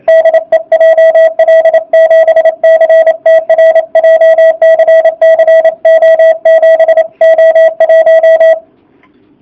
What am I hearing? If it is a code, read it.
Morse code, heard as a keyed tone, about 27 wpm, 650 Hz. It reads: DE JL7CTR JCC O7O1